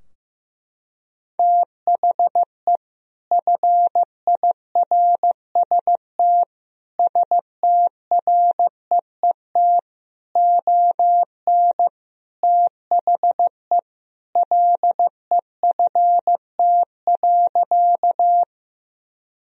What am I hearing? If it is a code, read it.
Morse code, heard as a keyed tone, about 15 wpm, 700 Hz. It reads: THE FIRST STREET ON THE LEFT.
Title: Sentences from Top 1000 Words - Review Part 3 - 15wpm